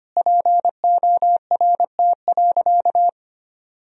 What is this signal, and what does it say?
Morse code, heard as a keyed tone, about 25 wpm, 700 Hz. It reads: PORT.
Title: Sentences from Top 1000 Words - Review Part 1 - 25wpm